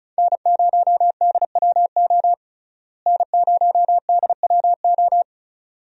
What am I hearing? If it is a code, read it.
Morse code, heard as a keyed tone, about 36 wpm, 700 Hz. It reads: N0DWO N0DWO
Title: USA Call Signs - 5 Characters - Repeated 2x - 35wpm